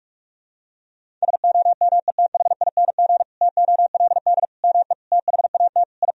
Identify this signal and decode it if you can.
Morse code, heard as a keyed tone, about 41 wpm, 700 Hz. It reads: SOMETHING TOLD ME THAT I
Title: Sentences from Top 1000 Words - Review Part 1 - 45wpm